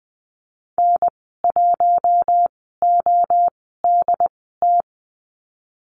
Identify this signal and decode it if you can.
Morse code, heard as a keyed tone, about 20 wpm, spaced as for 15 wpm, 700 Hz. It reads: N1ODT
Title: 128 Lesson 21g - Number 6 - Call Signs 20wpm